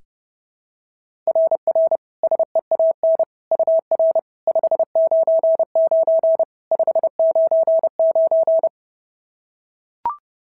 Morse code, heard as a keyed tone, about 30 wpm, 690 Hz. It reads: RR SEAN UR 599 599
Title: SOTA QSO Practice 30wpm